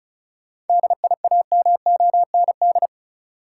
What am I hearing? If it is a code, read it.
Morse code, heard as a keyed tone, about 34 wpm, 700 Hz. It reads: DIAMOND
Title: Seven Letter Words - 35wpm